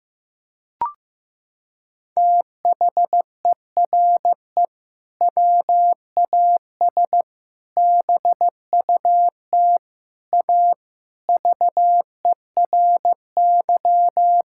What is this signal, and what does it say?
Morse code, heard as a keyed tone, about 15 wpm, 700 Hz. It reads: THERE WAS BUT A VERY
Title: Sentences from Top 1000 Words - Review Part 5 - 15wpm